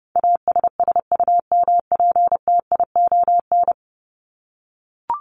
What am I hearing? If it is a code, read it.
Morse code, heard as a keyed tone, about 30 wpm, 700 Hz. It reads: ASSUMPTION
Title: Top 1751-2000 Words 30wpm